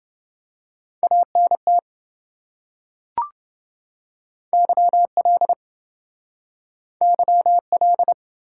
Morse code, heard as a keyed tone, about 27 wpm, 700 Hz. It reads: ANT YL YL